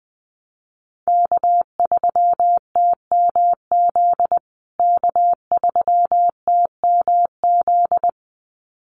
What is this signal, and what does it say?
Morse code, heard as a keyed tone, about 20 wpm, 700 Hz. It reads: K3TMZ K3TMZ
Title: USA Call Signs Repeated 2x 20wpm